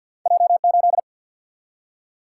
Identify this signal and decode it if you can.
Morse code, heard as a keyed tone, about 53 wpm, 700 Hz. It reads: J8
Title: Single-Letter Single-Number 50wpm